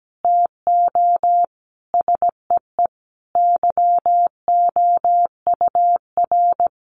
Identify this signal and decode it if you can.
Morse code, heard as a keyed tone, about 17 wpm, 700 Hz. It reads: TO SEE YOUR
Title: Sentences from Top 1500 Words 17wpm